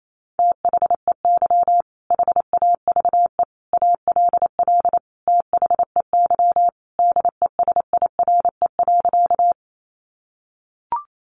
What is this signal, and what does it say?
Morse code, heard as a keyed tone, about 28 wpm, 700 Hz. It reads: THEY HAVE ALL THEY DESIRE.
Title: Sentences from Top 1100 Words 28wpm